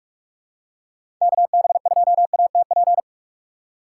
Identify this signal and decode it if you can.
Morse code, heard as a keyed tone, about 46 wpm, 700 Hz. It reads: KB2ATP
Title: USA Call Signs - 6 Characters - 3x Character Spacing - 45wpm